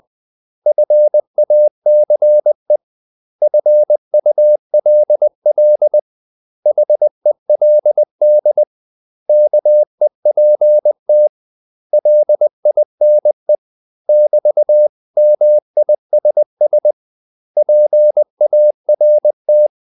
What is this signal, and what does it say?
Morse code, heard as a keyed tone, about 20 wpm, 600 Hz. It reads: FACE FULL HELD KEPT LINE <BT>MISS PART
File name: CW / Practice / 500MostCommonEnglishWords20